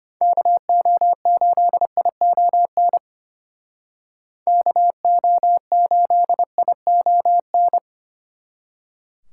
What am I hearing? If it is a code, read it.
Morse code, heard as a keyed tone, about 32 wpm, 700 Hz. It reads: KO8ION KO8ION